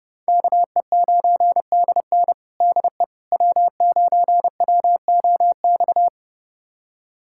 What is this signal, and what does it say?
Morse code, heard as a keyed tone, about 32 wpm, 700 Hz. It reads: KE9DN DE W9WOX
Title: USA Callsign DE Callsign - 5 Characters - 30wpm